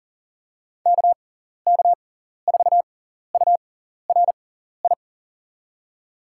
Morse code, heard as a keyed tone, about 40 wpm, spaced as for 14 wpm, 700 Hz. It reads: KK4URI
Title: USA Call Signs - 6 Characters - 6x Character Spacing - 40wpm